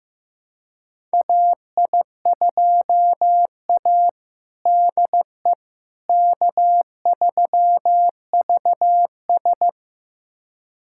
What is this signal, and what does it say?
Morse code, heard as a keyed tone, about 15 wpm, 700 Hz. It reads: AI2A DE K3VS
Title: USA Callsign DE Callsign - 4 Characters - 15wpm